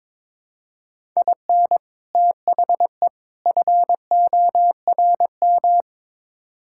Morse code, heard as a keyed tone, about 23 wpm, 700 Hz. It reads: IN THE FORM
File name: practice / 22wpm / Build A Phrase - 5-Word Phrases - 22wpm